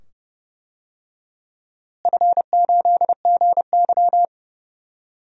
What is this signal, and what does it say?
Morse code, heard as a keyed tone, about 30 wpm, 700 Hz. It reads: F8GY
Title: DXCC-100 Call Signs - 4 Characters - Repeated 2x - 30wpm